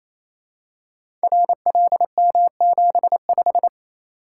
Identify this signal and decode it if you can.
Morse code, heard as a keyed tone, about 28 wpm, 700 Hz. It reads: RLM75
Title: Code Groups - 5 Letter:Numbers - 28wpm